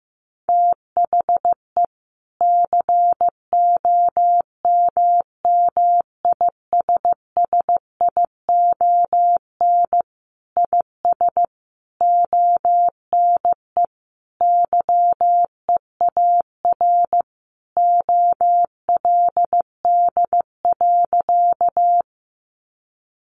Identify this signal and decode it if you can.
Morse code, heard as a keyed tone, about 14 wpm, 700 Hz. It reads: THE COMMISSION IS ONE YEAR OLD.
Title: Sentences from Top 1400 Words 15wpm